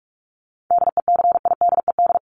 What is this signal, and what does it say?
Morse code, heard as a keyed tone, about 44 wpm, 700 Hz. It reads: DECIDED
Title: ED Suffix - Top 100 Words with Base Words 45wpm